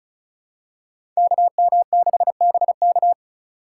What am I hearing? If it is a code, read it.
Morse code, heard as a keyed tone, about 36 wpm, 700 Hz. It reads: KM6BK
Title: USA Call Signs - 5 Characters - 4x Character Spacing - 35wpm